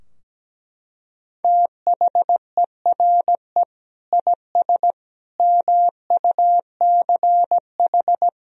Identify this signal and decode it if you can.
Morse code, heard as a keyed tone, about 17 wpm, 700 Hz. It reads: THERE IS MUCH
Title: Sentences from Top 1000 Words - Review Part 4 - 17wpm